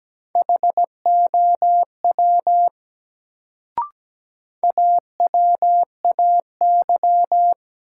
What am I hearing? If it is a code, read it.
Morse code, heard as a keyed tone, about 17 wpm, 700 Hz. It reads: HOW AWAY
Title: Sets of 2 Words - New Word Drill - 17wpm